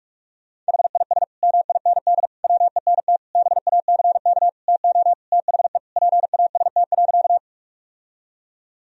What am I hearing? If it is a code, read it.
Morse code, heard as a keyed tone, about 46 wpm, 700 Hz. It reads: HIS MIND WENT BACK TO THE PAST.
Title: Sentences from Top 1000 Words - Review Part 3 - 45wpm